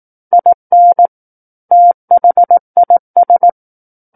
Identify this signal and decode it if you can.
Morse code, heard as a keyed tone, about 18 wpm, 700 Hz. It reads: IN THIS
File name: Book One / 09-Chapter 9 - 18 WPM